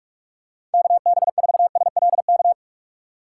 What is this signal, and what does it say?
Morse code, heard as a keyed tone, about 48 wpm, 700 Hz. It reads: KB4SLK
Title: USA Call Signs - 6 Characters - 3x Character Spacing - 45wpm